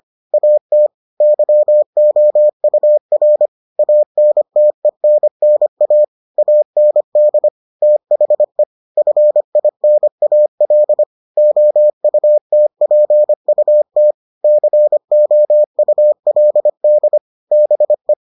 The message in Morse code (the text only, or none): AT YOUR ANTENNA AND THE FINAL OUTPUT COULD BE